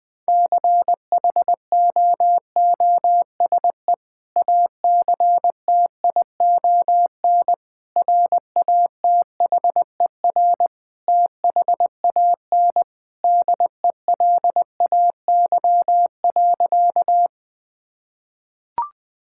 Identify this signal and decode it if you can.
Morse code, heard as a keyed tone, about 20 wpm, 700 Hz. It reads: CHOOSE ACTION RATHER THAN DELAY.
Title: Sentences with 2-Word Phrases - 20wpm